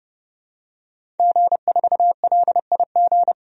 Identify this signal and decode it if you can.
Morse code, heard as a keyed tone, about 29 wpm, 700 Hz. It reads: G4LIG
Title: DXCC-100 Call Signs - 5 Characters - 30wpm